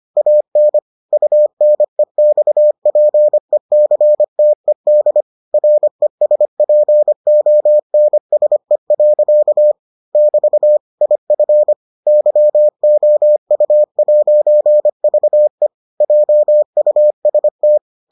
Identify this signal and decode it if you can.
Morse code, heard as a keyed tone, about 25 wpm, 600 Hz. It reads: AN UNEXPECTED RESPONSE. <BT> IF YOU'VE JUST